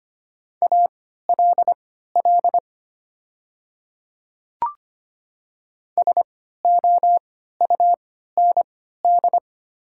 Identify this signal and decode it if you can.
Morse code, heard as a keyed tone, about 25 wpm, spaced as for 14 wpm, 700 Hz. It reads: ALL SOUND
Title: 3x Character Spacing - Top 100 Words - 25wpm